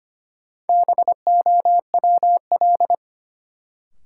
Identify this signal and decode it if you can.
Morse code, heard as a keyed tone, about 25 wpm, 700 Hz. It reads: BOWL
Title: Sentences from Top 1200 Words - New Word Drill - 25wpm